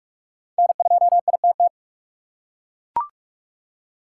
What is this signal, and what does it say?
Morse code, heard as a keyed tone, about 43 wpm, 700 Hz. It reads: N2ITT
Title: USA Call Signs - 5 Characters - 2x Character Spacing - 45wpm